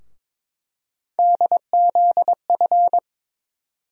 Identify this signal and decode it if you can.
Morse code, heard as a keyed tone, about 22 wpm, 700 Hz. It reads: DZF
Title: USA Call Sign Postfixes - 3 Letters 22wpm